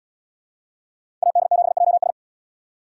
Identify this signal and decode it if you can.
Morse code, heard as a keyed tone, about 77 wpm, 700 Hz. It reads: AD7JS